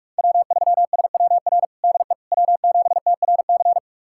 Morse code, heard as a keyed tone, about 42 wpm, 700 Hz. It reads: W3SWR DE W7TRC
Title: USA Callsign DE Callsign - 5 Characters - 45wpm